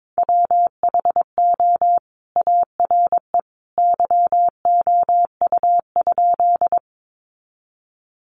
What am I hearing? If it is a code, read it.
Morse code, heard as a keyed tone, about 22 wpm, 700 Hz. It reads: WHO ARE YOU?
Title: Sentences from Top 100 Words 22wpm